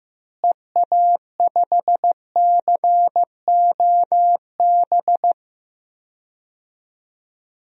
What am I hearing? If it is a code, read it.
Morse code, heard as a keyed tone, about 15 wpm, 700 Hz. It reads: EA5COB
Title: DXCC-100 Call Signs - 6 Characters - Repeated 3x - 15wpm